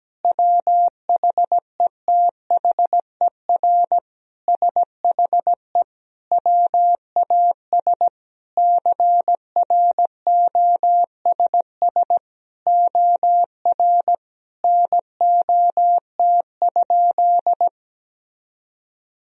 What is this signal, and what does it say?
Morse code, heard as a keyed tone, about 17 wpm, 700 Hz. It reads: WHETHER SHE WAS CROSS OR NOT?
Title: Sentences from Top 1000 Words - Review Part 3 - 17wpm